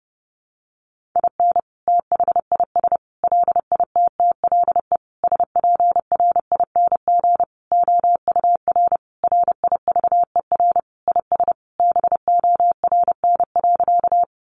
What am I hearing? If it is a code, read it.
Morse code, heard as a keyed tone, about 30 wpm, 700 Hz. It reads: IN THIS LITTLE SPRING OUR RIVER IS BORN.
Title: Sentences from Top 1200 Words 30wpm